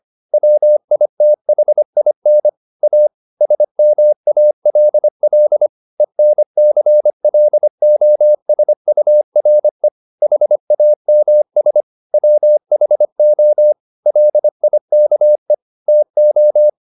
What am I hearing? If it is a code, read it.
Morse code, heard as a keyed tone, about 25 wpm, 600 Hz. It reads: WITHIN A SMALL ENCLOSURE HAMS WHO LIKE TO